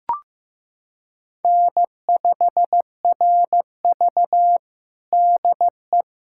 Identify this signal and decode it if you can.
Morse code, heard as a keyed tone, about 15 wpm, 700 Hz. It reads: N5RV DE
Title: USA Callsign DE Callsign - 4 Characters - 15wpm